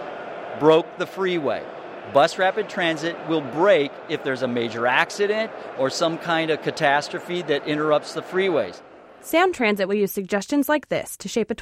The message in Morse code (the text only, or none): none